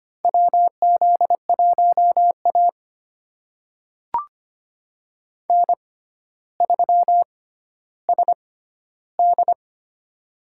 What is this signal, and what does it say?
Morse code, heard as a keyed tone, about 25 wpm, spaced as for 13 wpm, 700 Hz. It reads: WZ1A N3SD
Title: USA Call Signs - 4 Characters - 6x Character Spacing - 25wpm